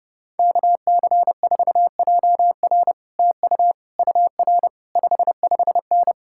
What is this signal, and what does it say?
Morse code, heard as a keyed tone, about 33 wpm, 700 Hz. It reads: KC4JR TU UR 55N